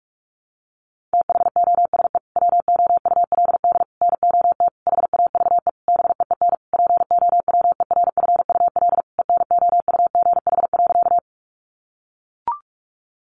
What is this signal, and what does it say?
Morse code, heard as a keyed tone, about 44 wpm, 700 Hz. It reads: THOSE WOULD NOT HAVE BEEN POWERFUL ENOUGH.
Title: Sentences from Top 1300 Words 45wpm